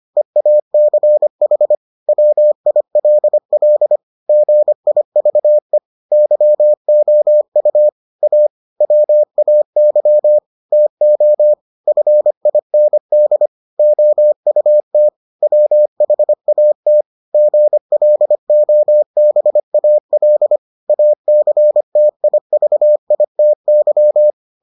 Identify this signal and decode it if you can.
Morse code, heard as a keyed tone, about 25 wpm, 600 Hz. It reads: EACH WILL GIVE YOU A WAY TO FIND OUT WHAT GLOBAL ACTIVITY